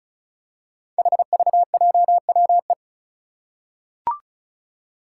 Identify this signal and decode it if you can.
Morse code, heard as a keyed tone, about 36 wpm, 700 Hz. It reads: HVJWE
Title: Five Challenging Letter-Number Combinations 35wpm